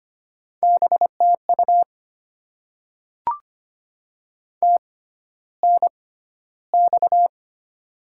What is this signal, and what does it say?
Morse code, heard as a keyed tone, about 25 wpm, spaced as for 13 wpm, 700 Hz. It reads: BTU TNX
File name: Top 20 QSO Elements - 6x Character Spacing - 25wpm